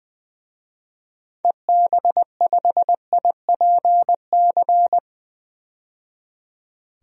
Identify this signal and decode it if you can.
Morse code, heard as a keyed tone, about 20 wpm, 700 Hz. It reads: EB5IPC
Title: DXCC-100 Call Signs - 6 Characters - 20wpm